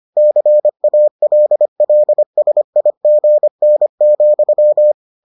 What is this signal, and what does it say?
Morse code, heard as a keyed tone, about 25 wpm, 600 Hz. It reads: CALLSIGN,